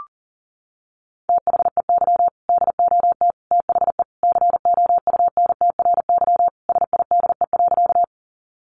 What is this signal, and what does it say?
Morse code, heard as a keyed tone, about 37 wpm, 700 Hz. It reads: THEY DOT THE COUNTRYSIDE.
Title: Sentences from Top 1500 Words 40wpm